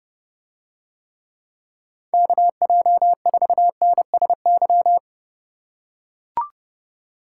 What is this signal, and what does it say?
Morse code, heard as a keyed tone, about 31 wpm, 700 Hz. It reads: KJ4NSY